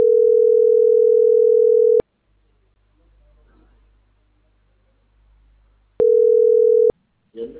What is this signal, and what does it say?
Ringback tone (North America), 440 + 480 Hz.